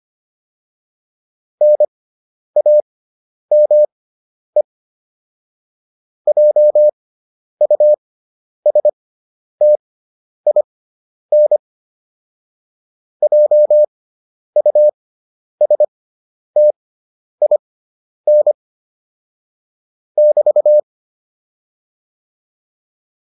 Morse code, heard as a keyed tone, about 25 wpm, spaced as for 10 wpm, 600 Hz. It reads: NAME JUSTIN JUSTIN <BT>